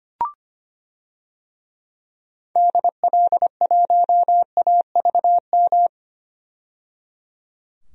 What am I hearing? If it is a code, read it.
Morse code, heard as a keyed tone, about 25 wpm, 700 Hz. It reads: DL1AVM
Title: DXCC-100 Call Signs - 6 Characters - 25wpm